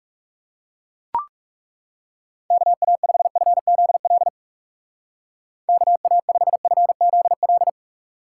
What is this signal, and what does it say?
Morse code, heard as a keyed tone, about 37 wpm, 700 Hz. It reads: KA5FZL KA5FZL
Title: Speed Racing - USA 2x3 Call Signs 30wpm